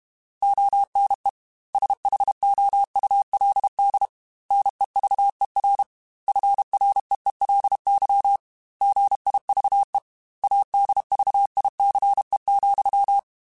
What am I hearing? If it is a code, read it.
Morse code, heard as a keyed tone, about 32 wpm, 800 Hz. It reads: ONE SHOULD NEVER FREELY GIVE ADVICE,
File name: Quotes / Quotes-mp3s / Advice @30